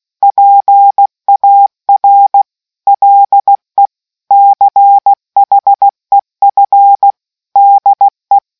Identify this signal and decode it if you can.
Morse code, heard as a keyed tone, about 16 wpm, 800 Hz. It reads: PAR LE CHEF DE